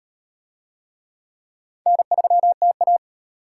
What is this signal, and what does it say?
Morse code, heard as a keyed tone, about 38 wpm, 700 Hz. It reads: N3TA